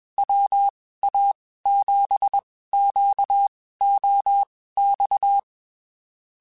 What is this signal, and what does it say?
Morse code, heard as a keyed tone, about 21 wpm, spaced as for 15 wpm, 800 Hz. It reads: WA7QOX
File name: (spaced) / QSOs Spaced-mp3s / WA7QOX 2 @20 sp